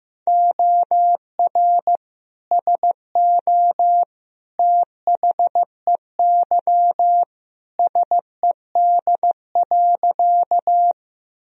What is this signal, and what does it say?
Morse code, heard as a keyed tone, about 16 wpm, 700 Hz. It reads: OR SO THEY SED.
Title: Sentences from Top 100 Words and Abbreviations 15wpm